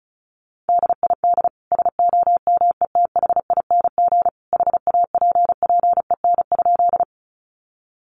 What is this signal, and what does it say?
Morse code, heard as a keyed tone, about 34 wpm, 700 Hz. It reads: DID SOMETHING HAPPEN?
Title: Sentences from Top 600 Words 35wpm